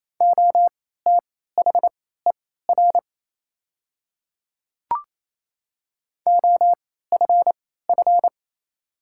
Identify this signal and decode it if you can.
Morse code, heard as a keyed tone, about 28 wpm, spaced as for 16 wpm, 700 Hz. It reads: OTHER OFF
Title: 3x Character Spacing - Top 200 Words - 28wpm